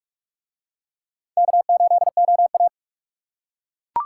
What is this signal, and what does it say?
Morse code, heard as a keyed tone, about 42 wpm, 700 Hz. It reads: K8OA